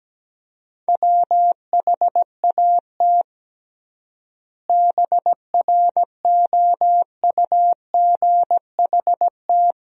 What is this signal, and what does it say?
Morse code, heard as a keyed tone, about 17 wpm, 700 Hz. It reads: WHAT BROUGHT
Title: 3x Word-Spacing - Sentences from Top 500 Words - 17wpm